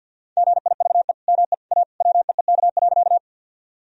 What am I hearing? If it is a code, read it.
Morse code, heard as a keyed tone, about 51 wpm, 700 Hz. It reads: GIVE ME A WEEK.
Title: Sentences from Top 1000 Words - Review Part 2 - 50wpm